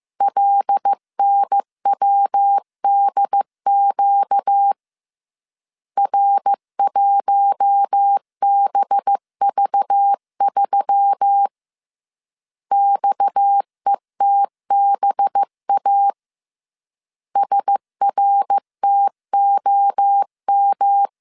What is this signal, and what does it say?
Morse code, heard as a keyed tone, about 15 wpm, 800 Hz. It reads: LNWDQ R1BV3 XETBA SRTOM